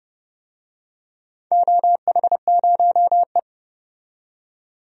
Morse code, heard as a keyed tone, about 30 wpm, 700 Hz. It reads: OH0E
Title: DXCC-100 Call Signs - 4 Characters - 30wpm